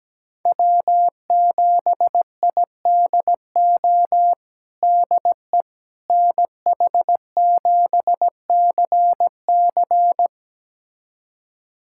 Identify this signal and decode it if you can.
Morse code, heard as a keyed tone, about 17 wpm, 700 Hz. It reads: W7IDO DE NH7CC
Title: USA Callsign DE Callsign - 5 Characters - 17wpm